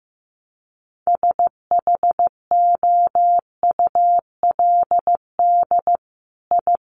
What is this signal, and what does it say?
Morse code, heard as a keyed tone, about 15 wpm, 700 Hz. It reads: SHOULD I
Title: Sentences from Top 900 Words 15wpm